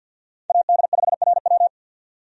Morse code, heard as a keyed tone, about 49 wpm, 700 Hz. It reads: AD5RW
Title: USA Call Signs - 5 Characters - 50wpm